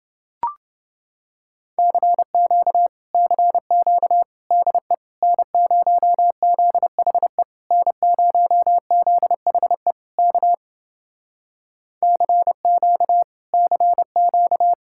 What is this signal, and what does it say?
Morse code, heard as a keyed tone, about 28 wpm, 700 Hz. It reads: CQ CQ DE N0ZHE N0ZHE K CQ CQ